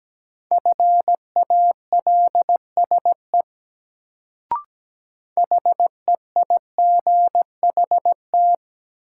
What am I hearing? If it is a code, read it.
Morse code, heard as a keyed tone, about 17 wpm, 700 Hz. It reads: FALSE HEIGHT
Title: Sentences from Top 1600 Words - New Word Drill - 17wpm